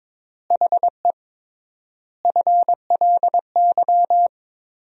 Morse code, heard as a keyed tone, about 22 wpm, 700 Hz. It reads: HE FLY
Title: 3x Word-Spacing - Sentences from Top 500 Words - 22wpm